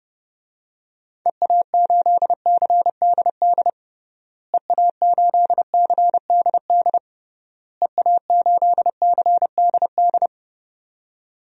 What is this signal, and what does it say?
Morse code, heard as a keyed tone, about 30 wpm, 700 Hz. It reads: EA8CDD EA8CDD EA8CDD
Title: DXCC-100 Call Signs - 6 Characters - Repeated 3x - 30wpm